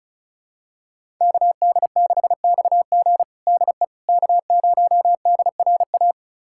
Morse code, heard as a keyed tone, about 36 wpm, 700 Hz. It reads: KD6XG DE K0DRA
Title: USA Callsign DE Callsign - 5 Characters - 35wpm